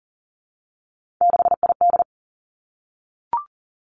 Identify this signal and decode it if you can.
Morse code, heard as a keyed tone, about 41 wpm, 690 Hz. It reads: 6ID
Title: Three Challenging Letter-Number Combinations 40wpm